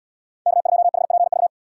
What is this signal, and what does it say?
Morse code, heard as a keyed tone, about 78 wpm, 700 Hz. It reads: K2HPV